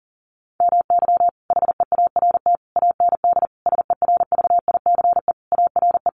Morse code, heard as a keyed tone, about 40 wpm, 700 Hz. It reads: MY HEART AND SERVICE ARE
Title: Sentences from Top 1200 Words 40wpm